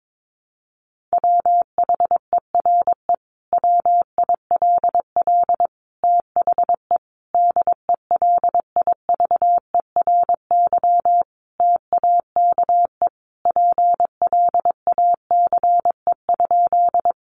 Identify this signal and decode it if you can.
Morse code, heard as a keyed tone, about 22 wpm, 700 Hz. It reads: WHERE WILL THE DELIVERY TAKE PLACE?
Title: Sentences from Top 1500 Words 22wpm